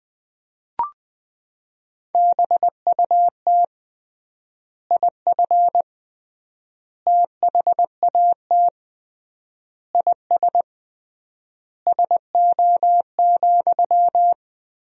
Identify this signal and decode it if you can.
Morse code, heard as a keyed tone, about 20 wpm, 700 Hz. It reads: BUT IF THAT IS SO,
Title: 3x Word-Spacing - Sentences from Top 200 Words - 20wpm